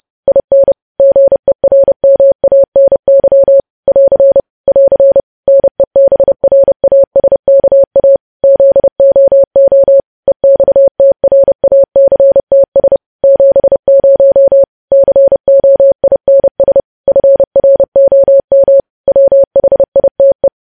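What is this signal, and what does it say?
Morse code, heard as a keyed tone, about 30 wpm, 550 Hz. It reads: IN GERMANY <AR> <AR> NEBRASKA ZOO EXTRACTS 70 COINS FROM WHITE
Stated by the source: Morse code news headlines practice, preceded by vvv.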